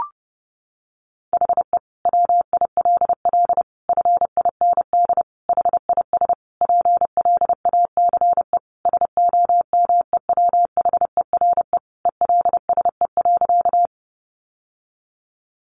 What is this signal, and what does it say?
Morse code, heard as a keyed tone, about 30 wpm, 700 Hz. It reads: HE WILL FIND HIS PLACE SOMEWHERE ELSE.
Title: Sentences from Top 1400 Words 30wpm